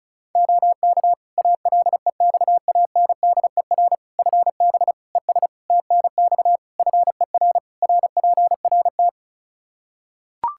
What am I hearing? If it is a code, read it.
Morse code, heard as a keyed tone, about 34 wpm, 700 Hz. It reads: OK ALEXANDER FB ES TNX FER RPRT
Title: Practice QSO - Exchange 2 - Info - 35wpm